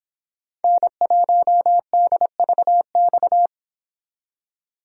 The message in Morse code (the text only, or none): N1DVX